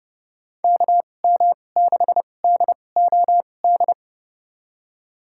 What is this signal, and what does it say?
Morse code, heard as a keyed tone, about 30 wpm, spaced as for 22 wpm, 700 Hz. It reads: KM6DOD
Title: USA Call Signs - 6 Characters - 2x Character Spacing - 30wpm